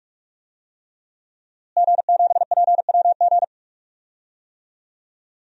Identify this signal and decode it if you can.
Morse code, heard as a keyed tone, about 42 wpm, 700 Hz. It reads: G7PWG
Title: DXCC-100 Call Signs - 5 Characters - Repeated 3x - 45wpm